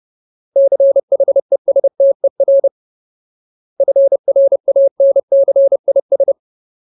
Morse code, heard as a keyed tone, about 30 wpm, 550 Hz. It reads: CHESTER FRANCIS
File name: Ultimate Word List / 30WPM Ultimate